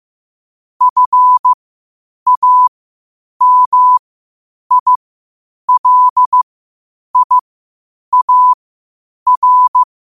Morse code, heard as a keyed tone, about 15 wpm, spaced as for 8 wpm, 1000 Hz. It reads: FAMILIAR
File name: qotdfiles / 8wpm